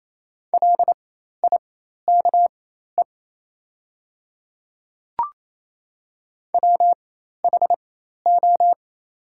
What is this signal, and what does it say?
Morse code, heard as a keyed tone, about 28 wpm, spaced as for 13 wpm, 700 Hz. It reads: LIKE WHO
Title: 4x Character Spacing - Top 100 Words - 28wpm